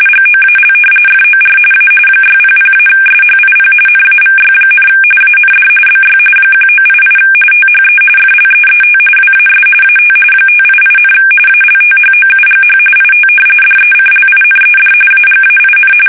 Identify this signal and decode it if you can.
RTTY, about 45.45 baud. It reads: GOAL FOR A 600-SHIP NAVY, AND BY 1988 THE FLEET WAS AT 588, ALTHOUGH IT DECLINED AGAIN
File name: fsk850 / usnavy5